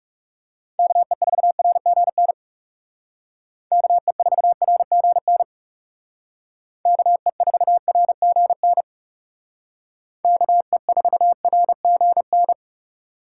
Morse code, heard as a keyed tone, about 39 wpm, 700 Hz. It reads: KE4RGN KE4RGN KE4RGN KE4RGN